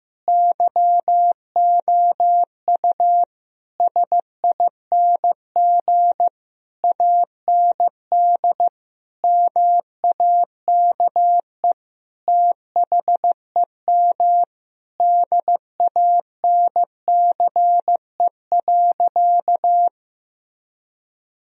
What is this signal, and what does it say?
Morse code, heard as a keyed tone, about 15 wpm, 700 Hz. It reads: YOU SING AND MAKE THEM DANCE.
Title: Sentences from Top 1000 Words - Review Part 4 - 15wpm